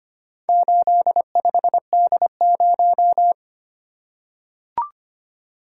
Morse code, heard as a keyed tone, about 25 wpm, 700 Hz. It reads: 85D0
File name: practice / 25wpm / Four Letter-Number Combinations - 25wpm